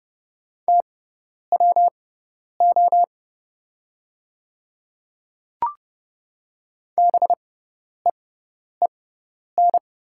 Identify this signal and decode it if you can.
Morse code, heard as a keyed tone, about 30 wpm, spaced as for 10 wpm, 700 Hz. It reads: TWO BEEN
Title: 6x Character Spacing - Top 300 Words - 30wpm